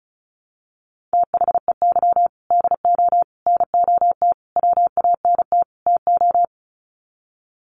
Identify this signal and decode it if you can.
Morse code, heard as a keyed tone, about 34 wpm, 700 Hz. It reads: THEY DO NOT WANT TO
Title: Sets of 5 Words 35wpm